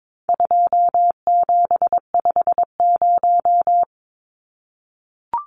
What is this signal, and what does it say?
Morse code, heard as a keyed tone, about 22 wpm, 700 Hz. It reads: 2750
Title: Numbers - 4-Digit 22wpm